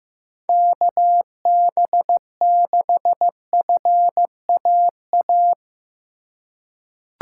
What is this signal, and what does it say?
Morse code, heard as a keyed tone, about 15 wpm, 700 Hz. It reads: KB6FAA
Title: USA Call Signs - 6 Characters - 2x Character Spacing - 15wpm